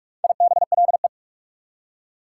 Morse code, heard as a keyed tone, about 60 wpm, 690 Hz. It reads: IBLE